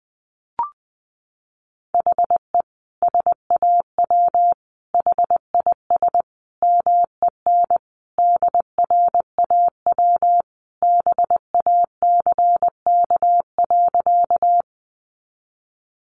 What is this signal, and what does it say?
Morse code, heard as a keyed tone, about 20 wpm, 700 Hz. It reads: HE SAW HIS MEN DRAW BACK.
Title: Sentences from Top 300 Words 20wpm